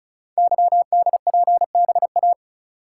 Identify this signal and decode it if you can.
Morse code, heard as a keyed tone, about 36 wpm, 700 Hz. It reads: YDPBA
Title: Code Groups - 5 Letters - 35wpm